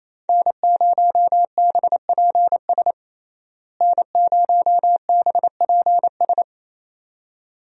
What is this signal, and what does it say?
Morse code, heard as a keyed tone, about 28 wpm, 700 Hz. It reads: N0BPS N0BPS
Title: USA Call Signs - 5 Characters - Repeated 2x - 28wpm